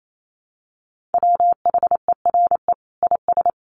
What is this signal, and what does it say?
Morse code, heard as a keyed tone, about 28 wpm, 700 Hz. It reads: WHERE IS
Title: Sentences from Top 1400 Words 28wpm